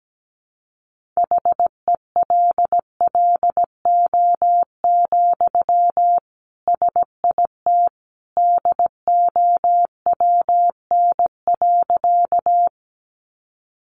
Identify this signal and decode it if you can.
Morse code, heard as a keyed tone, about 17 wpm, 700 Hz. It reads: HELLO, SIT DOWN.
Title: Sentences from Top 1100 Words 17wpm